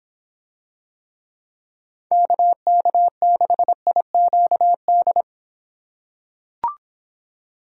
Morse code, heard as a keyed tone, about 26 wpm, 700 Hz. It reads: KK6IQD